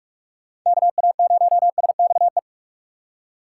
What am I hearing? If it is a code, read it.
Morse code, heard as a keyed tone, about 44 wpm, 700 Hz. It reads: KA0SXE
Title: USA Call Signs - 6 Characters - 3x Character Spacing - 45wpm